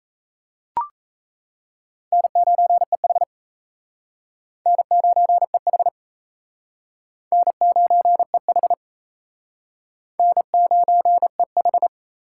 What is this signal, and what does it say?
Morse code, heard as a keyed tone, about 34 wpm, 700 Hz. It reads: N9EH N9EH N9EH N9EH
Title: Speed Racing - USA 1x2 Call Signs 28wpm